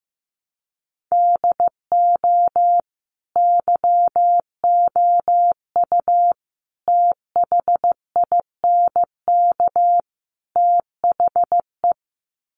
Morse code, heard as a keyed tone, about 15 wpm, 700 Hz. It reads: DO YOU THINK THE
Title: Sets of 4 Words 15wpm